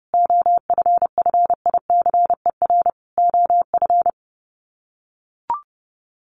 Morse code, heard as a keyed tone, about 28 wpm, 700 Hz. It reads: OFFICER OF
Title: Sets of 5 Words Encore2 30wpm